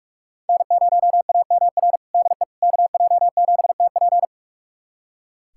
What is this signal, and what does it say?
Morse code, heard as a keyed tone, about 46 wpm, 700 Hz. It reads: N0AMR DE KJ7TP
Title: USA Callsign DE Callsign - 5 Characters - 45wpm